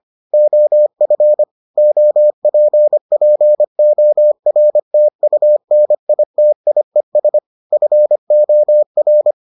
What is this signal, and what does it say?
Morse code, heard as a keyed tone, about 25 wpm, 600 Hz. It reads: OF OPPORTUNITIES FOR